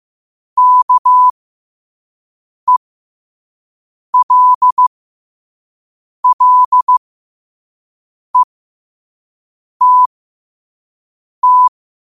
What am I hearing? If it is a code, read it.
Morse code, heard as a keyed tone, about 15 wpm, spaced as for 5 wpm, 1000 Hz. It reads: KELLETT
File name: qotdfiles / 5wpm